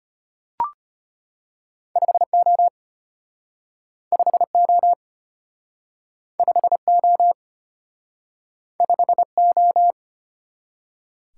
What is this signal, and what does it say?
Morse code, heard as a keyed tone, about 33 wpm, 700 Hz. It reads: 5O 5O 5O 5O